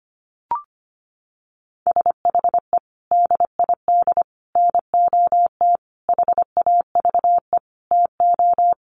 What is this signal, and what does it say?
Morse code, heard as a keyed tone, about 25 wpm, 700 Hz. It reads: SHE DID NOT HAVE TO
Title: Sets of 5 Words Encore2 25wpm